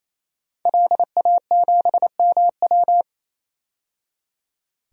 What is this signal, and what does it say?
Morse code, heard as a keyed tone, about 28 wpm, 700 Hz. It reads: LA7MW
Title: DXCC-100 Call Signs - 5 Characters - 28wpm